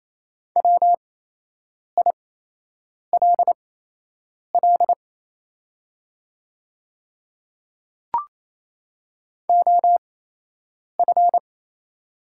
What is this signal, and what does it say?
Morse code, heard as a keyed tone, about 28 wpm, spaced as for 8 wpm, 690 Hz. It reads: WILL OF
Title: 8x Character Spacing - Top 100 Words - 28wpm